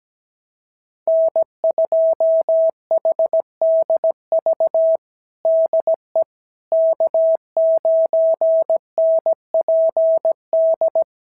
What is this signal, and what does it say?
Morse code, heard as a keyed tone, about 17 wpm, 650 Hz. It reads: N2HDV DE K9NPD
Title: SOTA QSO Practice 17wpm